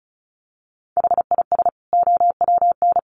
Morse code, heard as a keyed tone, about 34 wpm, 700 Hz. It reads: HIS OWN
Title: Sets of 2 Words 35wpm